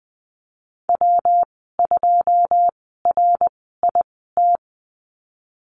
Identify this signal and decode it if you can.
Morse code, heard as a keyed tone, about 20 wpm, spaced as for 15 wpm, 700 Hz. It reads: W2RIT
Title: 122 Lesson 20g - Number 3 - Call Signs 20wpm